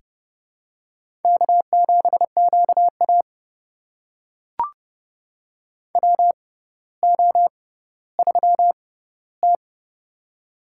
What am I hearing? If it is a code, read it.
Morse code, heard as a keyed tone, about 32 wpm, spaced as for 16 wpm, 700 Hz. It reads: K7QA WO3T
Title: USA Call Signs - 4 Characters - 6x Character Spacing - 30wpm